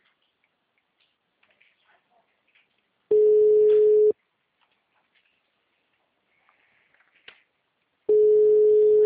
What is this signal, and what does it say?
Ringback tone (Europe, 425 Hz).
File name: out- -2014- - - .60